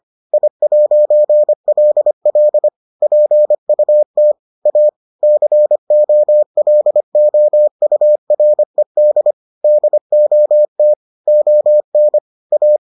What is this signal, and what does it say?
Morse code, heard as a keyed tone, about 25 wpm, 600 Hz. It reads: I'LL PUT A COLOURED DOT ON A